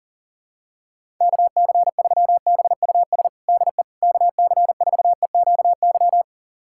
Morse code, heard as a keyed tone, about 40 wpm, 700 Hz. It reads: KC3BUS DE KC4EQY
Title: USA Callsign DE Callsign 40wpm